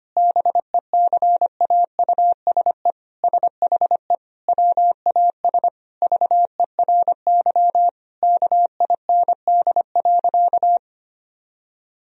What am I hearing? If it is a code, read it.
Morse code, heard as a keyed tone, about 25 wpm, 700 Hz. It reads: BECAUSE SHE WAS VERY KIND.
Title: Sentences from Top 1000 Words - Review Part 3 - 25wpm